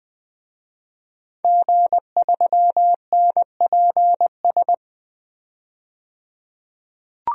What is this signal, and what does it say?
Morse code, heard as a keyed tone, about 20 wpm, 700 Hz. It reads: G3NPS
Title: DXCC-100 Call Signs - 5 Characters - 20wpm